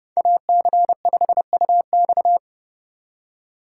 Morse code, heard as a keyed tone, about 26 wpm, 700 Hz. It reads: AC5UX